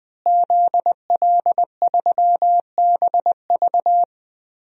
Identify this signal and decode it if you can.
Morse code, heard as a keyed tone, about 20 wpm, 700 Hz. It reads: ZL3BV
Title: DXCC-100 Call Signs - 5 Characters - Repeated 3x - 20wpm